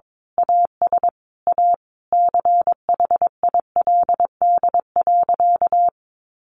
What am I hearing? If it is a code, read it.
Morse code, heard as a keyed tone, about 22 wpm, 700 Hz. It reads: AS A CHILD.
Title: Sentences from Top 1500 Words 22wpm